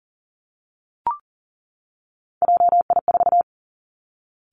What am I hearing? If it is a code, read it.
Morse code, heard as a keyed tone, about 38 wpm, 690 Hz. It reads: JI4
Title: Three Challenging Letter-Number Combinations 40wpm